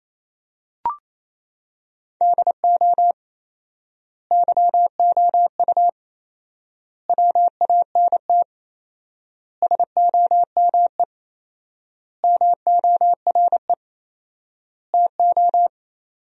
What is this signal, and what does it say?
Morse code, heard as a keyed tone, about 28 wpm, 700 Hz. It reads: DO YOU WANT SOME MORE TO
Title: 4x Word-Spacing - Sentences from Top 300 Words - 28wpm